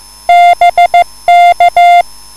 Morse code, heard as a keyed tone, about 14 wpm, 700 Hz. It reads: BK
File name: QSL / CW lesson